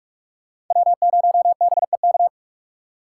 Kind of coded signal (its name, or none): Morse code